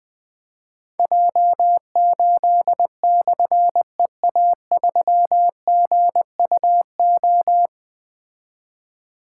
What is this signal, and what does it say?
Morse code, heard as a keyed tone, about 20 wpm, 700 Hz. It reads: J8/EA3GUO
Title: Difficult Call Signs 20wpm